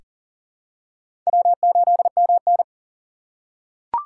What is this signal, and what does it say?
Morse code, heard as a keyed tone, about 40 wpm, 700 Hz. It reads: W8MN
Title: USA Call Signs - 3x Character Spacing - 40wpm